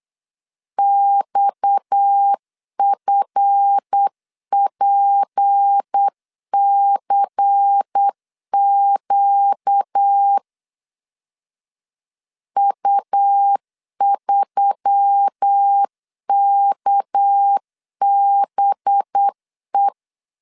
Morse code, heard as a keyed tone, about 8 wpm, 800 Hz. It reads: XFPCQ U3KBE
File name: LEZIONI / 7 GF9Z / 7 GF 8wpm 3108